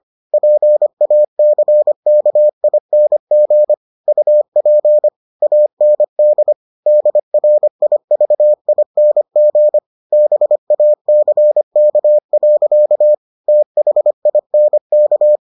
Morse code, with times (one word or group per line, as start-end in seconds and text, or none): 0.3-3.7 PACKING
4.1-5.1 UP
5.4-6.5 AND
6.9-9.8 DRIVING
10.1-13.1 BACK.
13.5-15.4 THINK